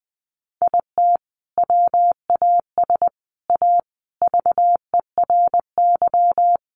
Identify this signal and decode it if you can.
Morse code, heard as a keyed tone, about 20 wpm, 700 Hz. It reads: IT WAS A VERY
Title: Sentences from Top 500 Words 20wpm